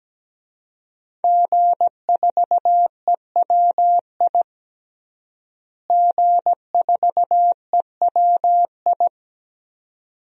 Morse code, heard as a keyed tone, about 17 wpm, 700 Hz. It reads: G4EWI G4EWI
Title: DXCC-100 Call Signs - 5 Characters - Repeated 2x - 17wpm